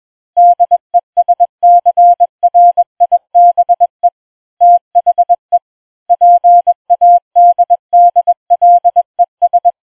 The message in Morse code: DESCRIBE THE PADDLES